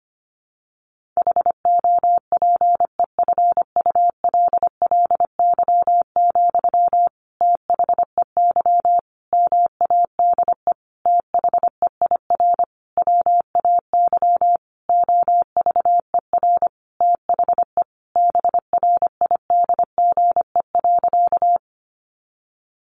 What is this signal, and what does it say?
Morse code, heard as a keyed tone, about 25 wpm, 700 Hz. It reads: HOPEFULLY, THEY MADE THEIR WAY OVER THE BRIDGE.
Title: Sentences from Top 1500 Words 25wpm